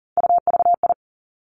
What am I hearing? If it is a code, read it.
Morse code, heard as a keyed tone, about 38 wpm, 700 Hz. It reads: UVI
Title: Three Challenging Letter-Number Combinations 40wpm